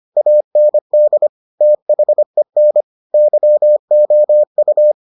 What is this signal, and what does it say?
Morse code, heard as a keyed tone, about 25 wpm, 600 Hz. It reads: AND THEN YOU